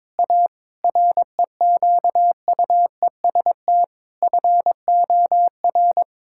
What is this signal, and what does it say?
Morse code, heard as a keyed tone, about 22 wpm, 700 Hz. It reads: A REQUEST FOR
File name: practice / 22wpm / Sets of 8 Words - 22wpm